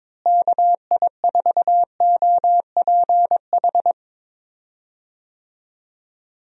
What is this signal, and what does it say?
Morse code, heard as a keyed tone, about 21 wpm, 700 Hz. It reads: KI4OPH
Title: USA Call Signs - 6 Characters - Repeated 2x - 22wpm